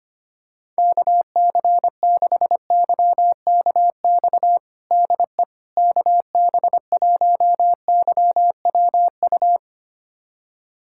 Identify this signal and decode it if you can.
Morse code, heard as a keyed tone, about 25 wpm, 700 Hz. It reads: KC6YKX DE KB1YWU
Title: USA Callsign DE Callsign - 6 Characters - 25wpm